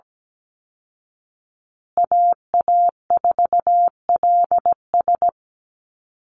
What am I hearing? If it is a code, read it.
Morse code, heard as a keyed tone, about 17 wpm, 700 Hz. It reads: AA4LS
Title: Call Signs Repeated 2x - 17wpm